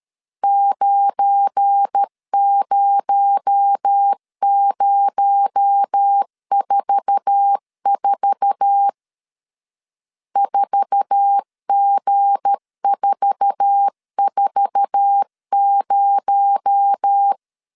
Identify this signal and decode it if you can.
Morse code, heard as a keyed tone, about 13 wpm, 800 Hz. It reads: 90044 4G440
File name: LEZIONI / 8 YJ45 / 8 YJ4 12wpm 2151